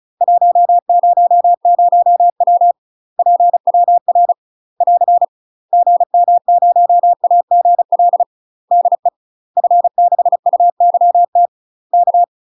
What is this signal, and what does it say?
Morse code, heard as a keyed tone, about 36 wpm, 700 Hz. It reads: 100W PWR <AR> GM0AGL DE F6UYT K
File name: UK MASTER QSOs TRAINING / 35WPM / UK MASTER QSOs 0007